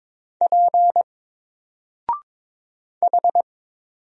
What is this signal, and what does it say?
Morse code, heard as a keyed tone, about 22 wpm, 700 Hz. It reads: P H
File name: Single-Letter-Number-22wpm